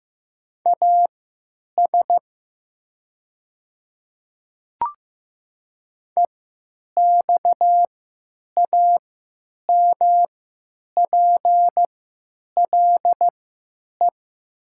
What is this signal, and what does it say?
Morse code, heard as a keyed tone, about 15 wpm, spaced as for 9 wpm, 700 Hz. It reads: AS EXAMPLE
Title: 3x Character Spacing - Top 300 Words - 15wpm